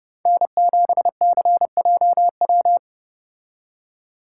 Morse code, heard as a keyed tone, about 29 wpm, 700 Hz. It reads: N7CJW